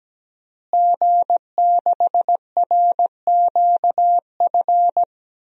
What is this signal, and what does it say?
Morse code, heard as a keyed tone, about 17 wpm, 700 Hz. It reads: G6RQF
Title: DXCC-100 Call Signs - 5 Characters - Repeated 3x - 17wpm